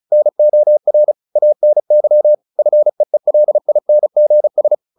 Morse code, heard as a keyed tone, about 34 wpm, 600 Hz. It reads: NOR ANY FEELINGS
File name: the old man and the sea 2 35wpm